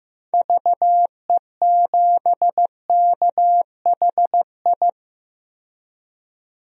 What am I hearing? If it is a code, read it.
Morse code, heard as a keyed tone, about 15 wpm, 700 Hz. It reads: VE7KHI
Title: DXCC-100 Call Signs - 6 Characters - 15wpm